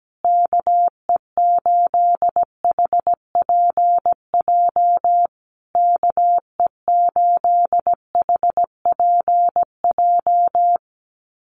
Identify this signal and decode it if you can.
Morse code, heard as a keyed tone, about 17 wpm, 700 Hz. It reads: KE8HPJ KE8HPJ
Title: USA Call Signs Repeated 3x 17wpm